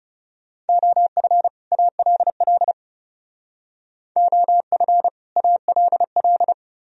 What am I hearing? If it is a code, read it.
Morse code, heard as a keyed tone, about 32 wpm, 700 Hz. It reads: OF ALL OF ALL
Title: Speed-Racing - Sets of 2 Words 30wpm